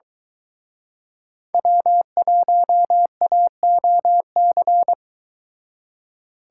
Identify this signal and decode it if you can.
Morse code, heard as a keyed tone, about 23 wpm, 700 Hz. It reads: W1AOC